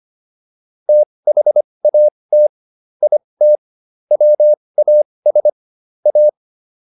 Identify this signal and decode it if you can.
Morse code, heard as a keyed tone, about 25 wpm, spaced as for 20 wpm, 600 Hz. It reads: THAT IT WAS A